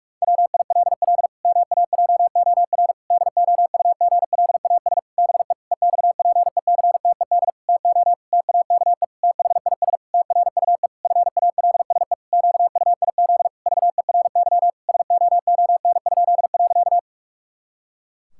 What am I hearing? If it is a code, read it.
Morse code, heard as a keyed tone, about 46 wpm, 700 Hz. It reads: WILL MAJOR DOUGLAS BE EXPECTED TO TAKE THIS TRUE FALSE QUIZ VERY SOON?.
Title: Pangram Challenge v2 - 45wpm